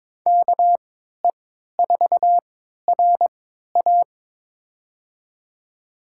Morse code, heard as a keyed tone, about 21 wpm, spaced as for 12 wpm, 700 Hz. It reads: KE4RA
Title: USA Call Signs - 5 Characters - 3x Character Spacing - 22wpm